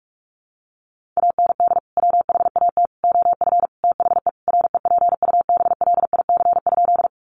Morse code, heard as a keyed tone, about 44 wpm, 700 Hz. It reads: AND WHAT OF THE REPUBLIC?
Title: Sentences from Top 1500 Words 45wpm